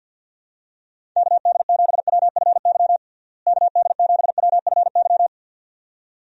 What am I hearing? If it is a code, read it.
Morse code, heard as a keyed tone, about 53 wpm, 700 Hz. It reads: KD7WFY KD7WFY
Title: USA Call Signs - 6 Characters - Repeated 2x - 50wpm